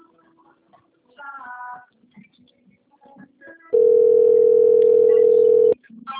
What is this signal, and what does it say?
Ringback tone (North America), 440 + 480 Hz.